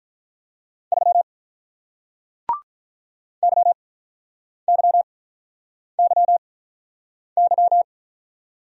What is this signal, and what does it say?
Morse code, heard as a keyed tone, about 46 wpm, spaced as for 9 wpm, 700 Hz. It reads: 3 YYYY